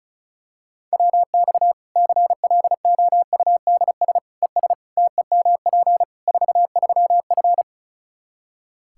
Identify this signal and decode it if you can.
Morse code, heard as a keyed tone, about 34 wpm, 700 Hz. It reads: WX CLOUDS ES TEMP 43F
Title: Practice QSO - Exchange 2 - Info - 35wpm